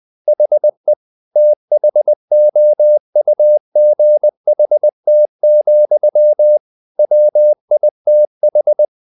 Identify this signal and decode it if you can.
Morse code, heard as a keyed tone, about 20 wpm, 600 Hz. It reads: HE THOUGHT, WITH
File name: the old man and the sea / the old man and the sea part 1 20wpm